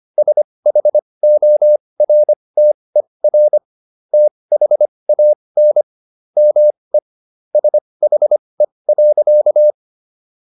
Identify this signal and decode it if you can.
Morse code, heard as a keyed tone, about 25 wpm, spaced as for 20 wpm, 600 Hz. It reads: SHORTER THAN ME SHE.
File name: SS123 20